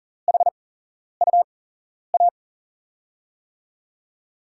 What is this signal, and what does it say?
Morse code, heard as a keyed tone, about 38 wpm, spaced as for 11 wpm, 700 Hz. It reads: HUA
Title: USA Call Signs - 5 Characters - 8x Character Spacing - 40wpm